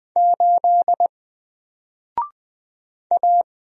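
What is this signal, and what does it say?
Morse code, heard as a keyed tone, about 20 wpm, 700 Hz. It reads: 8 A
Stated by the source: Single letter and number Morse drill at 20 wpm, from N4LCD.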